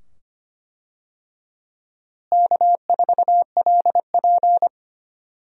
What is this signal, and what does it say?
Morse code, heard as a keyed tone, about 25 wpm, 700 Hz. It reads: K4LP
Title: USA Call Signs - 4 Characters - Repeated 2x - 25wpm